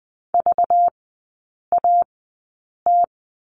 Morse code, heard as a keyed tone, about 20 wpm, spaced as for 8 wpm, 700 Hz. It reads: VAT
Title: 205 Lesson 33e - Letter j - Three Characters 20wpm